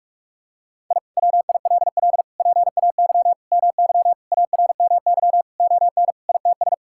Morse code, heard as a keyed tone, about 46 wpm, 700 Hz. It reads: I WILL PAY MY ARMY ON ITS
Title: Sentences from Top 1000 Words - Review Part 5 - 45wpm